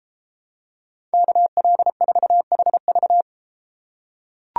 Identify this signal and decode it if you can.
Morse code, heard as a keyed tone, about 33 wpm, 700 Hz. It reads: KL4HV